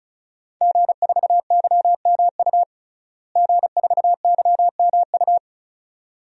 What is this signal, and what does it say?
Morse code, heard as a keyed tone, about 34 wpm, 700 Hz. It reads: G4YMU G4YMU
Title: DXCC-100 Call Signs - 5 Characters - Repeated 2x - 35wpm